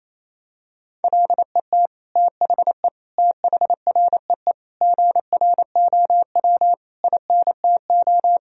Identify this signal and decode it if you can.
Morse code, heard as a keyed tone, about 28 wpm, 700 Hz. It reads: LET THE THREE GROW INTO
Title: Sentences from Top 1000 Words - Review Part 1 - 28wpm